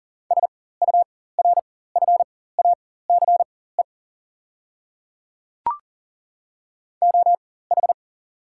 Morse code, heard as a keyed tone, about 38 wpm, spaced as for 18 wpm, 690 Hz. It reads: SURFACE OH
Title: 4x Character Spacing - Top 500 Words - 40wpm